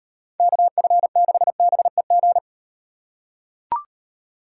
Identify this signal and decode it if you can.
Morse code, heard as a keyed tone, about 35 wpm, 700 Hz. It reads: KF6BEG